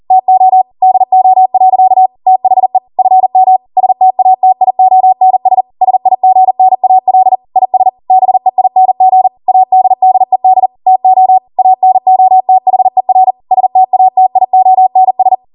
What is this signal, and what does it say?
Morse code, heard as a keyed tone, about 40 wpm, 750 Hz. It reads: TO DO. THE FM STATIONS SIGNAL IS BEING ADDED TO ANOTHER STATIONS
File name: Morse / 40WPM